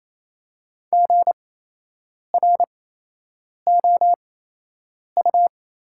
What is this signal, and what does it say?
Morse code, heard as a keyed tone, about 28 wpm, spaced as for 8 wpm, 690 Hz. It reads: GROU
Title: 8x Character Spacing - Top 500 Words - 28wpm